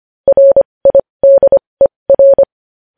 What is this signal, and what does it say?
Morse code, heard as a keyed tone, about 25 wpm, spaced as for 20 wpm, 550 Hz. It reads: RIDER
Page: News Headlines in Morse Code at 20 WPM